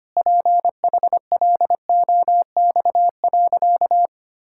Morse code, heard as a keyed tone, about 25 wpm, 700 Hz. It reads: PHLOX.
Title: Pangram Challenge v2 - 25wpm